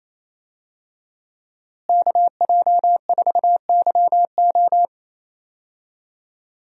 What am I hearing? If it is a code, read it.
Morse code, heard as a keyed tone, about 28 wpm, 700 Hz. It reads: KJ4YO
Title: USA Call Signs - 5 Characters - Repeated 2x - 28wpm